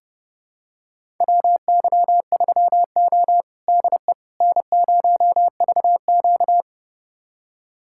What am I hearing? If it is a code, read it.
Morse code, heard as a keyed tone, about 30 wpm, 700 Hz. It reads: WY3O DE N0VQ
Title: USA Callsign DE Callsign - 4 Characters - 30wpm